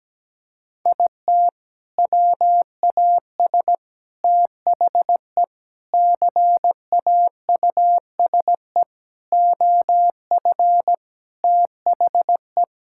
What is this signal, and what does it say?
Morse code, heard as a keyed tone, about 17 wpm, 700 Hz. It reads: IT WAS THE CAUSE OF THE
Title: Sentences from Top 1600 Words 17wpm